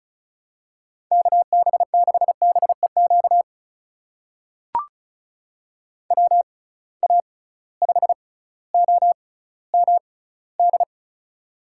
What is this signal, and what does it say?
Morse code, heard as a keyed tone, about 36 wpm, spaced as for 18 wpm, 700 Hz. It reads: KB6BEQ WA5OMD
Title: USA Call Signs - 6 Characters - 6x Character Spacing - 35wpm